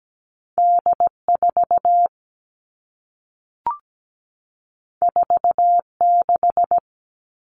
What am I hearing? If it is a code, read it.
Morse code, heard as a keyed tone, about 17 wpm, 700 Hz. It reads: D4 46
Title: All Cut Numbers - 2-Digit 17wpm